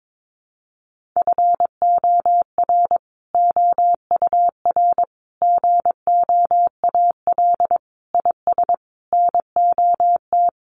Morse code, heard as a keyed tone, about 22 wpm, 700 Hz. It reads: FOR OUR GOAL IS NOT
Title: Sentences from Top 1400 Words 22wpm